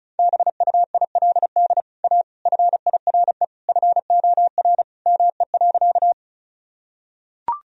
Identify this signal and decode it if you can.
Morse code, heard as a keyed tone, about 36 wpm, 700 Hz. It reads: BUILD A FIRE FOR ME.
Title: Sentences from Top 1000 Words - Review Part 1 - 35wpm